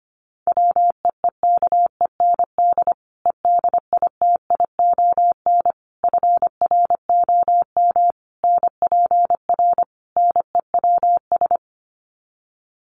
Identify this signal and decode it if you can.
Morse code, heard as a keyed tone, about 25 wpm, 700 Hz. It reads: WEEKEND EDITION FROM NPR NEWS
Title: Sets of 5 Words 25wpm